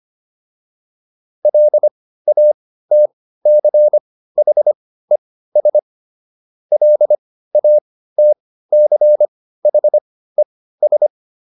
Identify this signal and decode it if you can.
Morse code, heard as a keyed tone, about 25 wpm, spaced as for 15 wpm, 600 Hz. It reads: LATCHES LATCHES